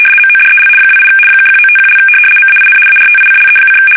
RTTY, about 75 baud. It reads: MODEL 28 (1951) MAIN ARTICLE:
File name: fsk850-75B / teletype-corp